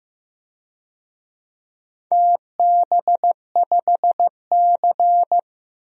Morse code, heard as a keyed tone, about 15 wpm, 700 Hz. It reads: TB5C
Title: DXCC-100 Call Signs - 4 Characters - 15wpm